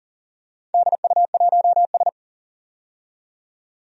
Morse code, heard as a keyed tone, about 40 wpm, 700 Hz. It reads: DU1S
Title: DXCC-100 Call Signs - 4 Characters - 40wpm